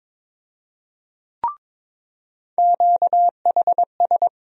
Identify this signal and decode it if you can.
Morse code, heard as a keyed tone, about 22 wpm, 700 Hz. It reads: QHS
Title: USA Call Sign Postfixes - 3 Letters 22wpm